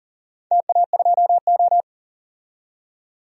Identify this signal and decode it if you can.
Morse code, heard as a keyed tone, about 38 wpm, 700 Hz. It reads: TA2O
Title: DXCC-100 Call Signs - 4 Characters - 40wpm